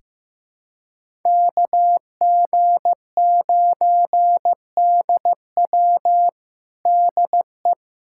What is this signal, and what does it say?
Morse code, heard as a keyed tone, about 15 wpm, 700 Hz. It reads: KG9DW DE
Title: USA Callsign DE Callsign 15wpm